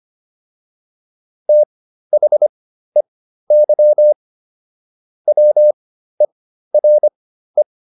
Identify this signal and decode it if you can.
Morse code, heard as a keyed tone, about 25 wpm, spaced as for 13 wpm, 600 Hz. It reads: THEY WERE